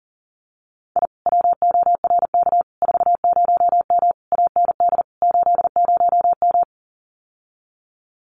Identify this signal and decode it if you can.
Morse code, heard as a keyed tone, about 40 wpm, 700 Hz. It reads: I WORK 40M AND 80M
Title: Common QSO Phrases Encore 40wpm